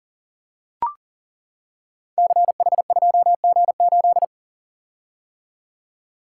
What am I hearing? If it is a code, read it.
Morse code, heard as a keyed tone, about 42 wpm, 700 Hz. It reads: CH2G8
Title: Code Groups - 5 Letter:Numbers - 40wpm